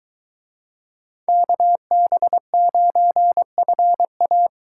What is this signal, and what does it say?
Morse code, heard as a keyed tone, about 22 wpm, 700 Hz. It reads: KB9FA